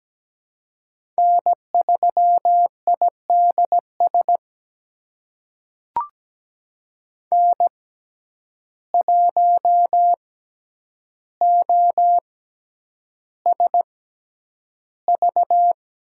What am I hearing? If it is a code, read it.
Morse code, heard as a keyed tone, about 17 wpm, spaced as for 9 wpm, 700 Hz. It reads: N3IDS N1OSV
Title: USA Call Signs - 5 Characters - 6x Character Spacing - 17wpm